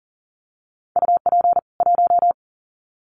Morse code, heard as a keyed tone, about 40 wpm, 700 Hz. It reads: UP 1
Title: Common QSO Phrases 40wpm